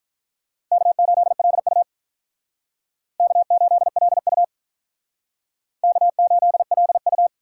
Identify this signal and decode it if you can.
Morse code, heard as a keyed tone, about 45 wpm, 700 Hz. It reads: K8LU K8LU K8LU